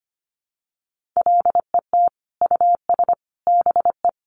Morse code, heard as a keyed tone, about 25 wpm, 700 Hz. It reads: LET US BE
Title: Sentences from Top 1400 Words 25wpm